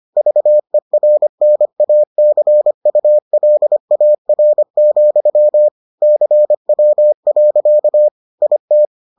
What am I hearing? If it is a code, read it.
Morse code, heard as a keyed tone, about 25 wpm, 600 Hz. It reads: VERNACULAR, CW. IT